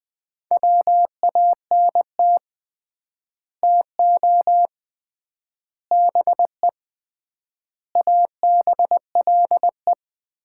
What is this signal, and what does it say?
Morse code, heard as a keyed tone, about 20 wpm, 700 Hz. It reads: WANT TO BE ABLE
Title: 3x Word-Spacing - Sets of 5 Words - 20wpm